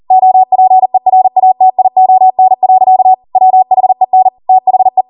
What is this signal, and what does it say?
Morse code, heard as a keyed tone, about 40 wpm, 750 Hz. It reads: OPERATION. WHEN THE